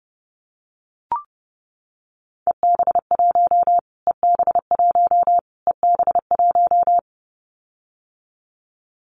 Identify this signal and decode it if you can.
Morse code, heard as a keyed tone, about 30 wpm, 700 Hz. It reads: EB1 EB1 EB1
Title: Callsign Prefixes for Top 25 DXCC - Repeated 3x - 30wpm